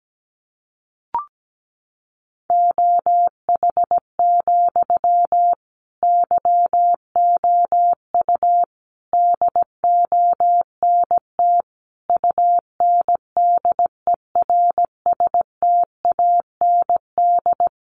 Morse code, heard as a keyed tone, about 17 wpm, 700 Hz. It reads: OH, YOU DONT UNDERSTAND
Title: Sentences from Top 1000 Words 17wpm